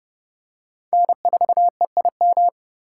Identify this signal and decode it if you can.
Morse code, heard as a keyed tone, about 32 wpm, 700 Hz. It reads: N4EIM